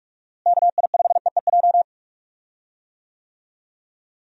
Morse code, heard as a keyed tone, about 46 wpm, 700 Hz. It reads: KI5EEJ
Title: USA Call Signs - 6 Characters - Repeated 3x - 45wpm